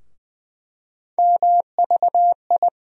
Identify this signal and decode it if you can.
Morse code, heard as a keyed tone, about 20 wpm, 700 Hz. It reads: MVI
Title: USA Call Sign Postfixes - 3 Letters 20wpm